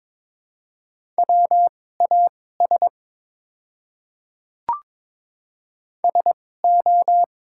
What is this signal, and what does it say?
Morse code, heard as a keyed tone, about 22 wpm, spaced as for 16 wpm, 690 Hz. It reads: WAS SO
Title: 2x Character Spacing - Top 100 Words - 22wpm